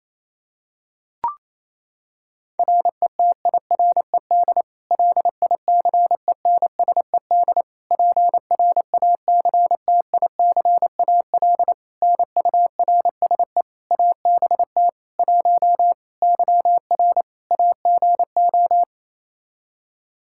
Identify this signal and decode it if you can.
Morse code, heard as a keyed tone, about 28 wpm, 700 Hz. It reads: RETIRED LICENSED PRACTICAL NURSE ABT 1 YR AGO
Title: Practice QSO - Exchange 3 - More Info - 28wpm